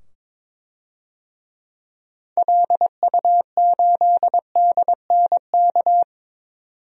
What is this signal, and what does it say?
Morse code, heard as a keyed tone, about 22 wpm, 700 Hz. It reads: LU8DNK
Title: DXCC-100 Call Signs - 6 Characters - 22wpm